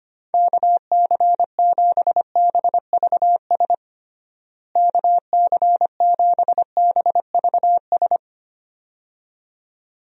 Morse code, heard as a keyed tone, about 25 wpm, 700 Hz. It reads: KC7BVS KC7BVS
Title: USA Call Signs - 6 Characters - Repeated 2x - 25wpm